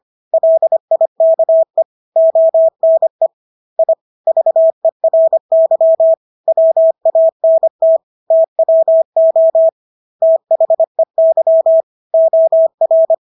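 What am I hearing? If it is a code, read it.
Morse code, heard as a keyed tone, about 25 wpm, 650 Hz. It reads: LIKE ONE I VERY WANT TWO THEY OR
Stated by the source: Morse practice of English words at 25 wpm from CW med Gnister.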